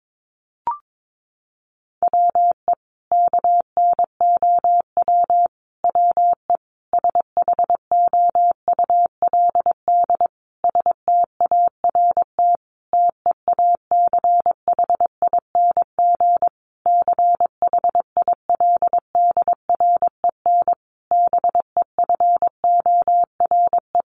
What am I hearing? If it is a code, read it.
Morse code, heard as a keyed tone, about 22 wpm, 700 Hz. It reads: WE KNOW WE SHOULD START TEACHING CHILDREN BEFORE
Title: Sentences from Top 1100 Words 22wpm